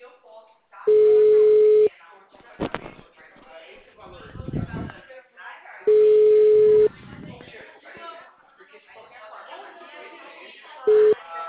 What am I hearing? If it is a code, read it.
Ringback tone (Europe, 425 Hz), with speech.
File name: out- -2009- - - .406